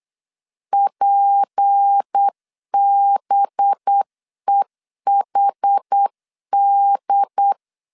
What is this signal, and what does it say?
Morse code, heard as a keyed tone, about 8 wpm, 800 Hz. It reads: PBEHD
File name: LEZIONI / 7 GF9Z / 7 G 8wpm 3118